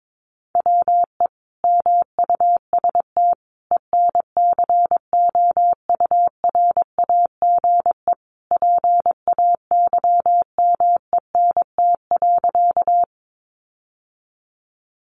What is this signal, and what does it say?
Morse code, heard as a keyed tone, about 22 wpm, 700 Hz. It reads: WE MUST ENCOURAGE PAYMENT.
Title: Sentences from Top 1400 Words 22wpm